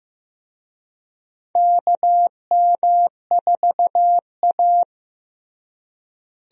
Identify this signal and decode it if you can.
Morse code, heard as a keyed tone, about 16 wpm, 700 Hz. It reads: KM4A
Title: USA Call Signs - 4 Characters - Repeated 2x - 15wpm